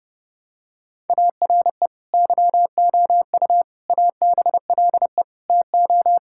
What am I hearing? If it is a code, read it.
Morse code, heard as a keyed tone, about 30 wpm, 700 Hz. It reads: ARE YOU ABLE TO
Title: Sentences from Top 1000 Words - Review Part 5 - 30wpm